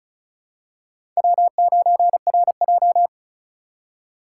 Morse code, heard as a keyed tone, about 36 wpm, 700 Hz. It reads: W9RJ
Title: USA Call Signs - 4 Characters - 3x Character Spacing - 35wpm